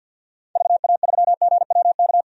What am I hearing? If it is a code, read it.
Morse code, heard as a keyed tone, about 46 wpm, 700 Hz. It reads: VA3GWK
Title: DXCC-100 Call Signs - 6 Characters - 50wpm